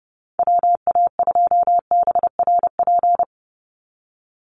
Morse code, heard as a keyed tone, about 30 wpm, 700 Hz. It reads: WA2BRP
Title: USA Call Signs 30wpm